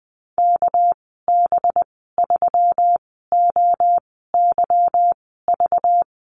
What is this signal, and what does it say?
Morse code, heard as a keyed tone, about 19 wpm, spaced as for 14 wpm, 700 Hz. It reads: KB3OYV
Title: 238 Lesson 38g - Letter q - Call Signs 20wpm